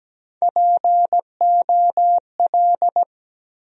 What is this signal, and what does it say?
Morse code, heard as a keyed tone, about 17 wpm, 700 Hz. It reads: POL
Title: USA Call Sign Postfixes - 3 Letters 17wpm